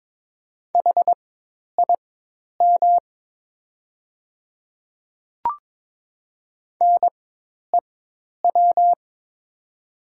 Morse code, heard as a keyed tone, about 22 wpm, spaced as for 10 wpm, 700 Hz. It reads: HIM NEW
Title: 4x Character Spacing - Top 200 Words - 22wpm